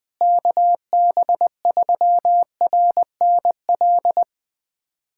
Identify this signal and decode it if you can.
Morse code, heard as a keyed tone, about 20 wpm, 700 Hz. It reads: KB3RNL
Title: USA Callsign DE Callsign 20wpm